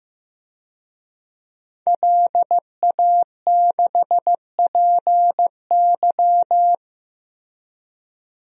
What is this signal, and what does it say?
Morse code, heard as a keyed tone, about 15 wpm, 700 Hz. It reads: LA6PY
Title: DXCC-100 Call Signs - 5 Characters - Repeated 2x - 15wpm